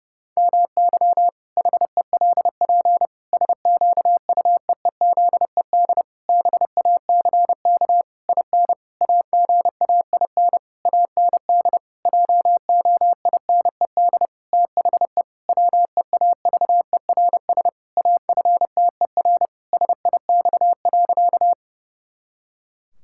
Morse code, heard as a keyed tone, about 32 wpm, 700 Hz. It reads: MY HELP SQUEEZED BACK IN AGAIN AND JOINED THE WEAVERS AFTER SIX.
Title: Pangram Challenge v2 - 30wpm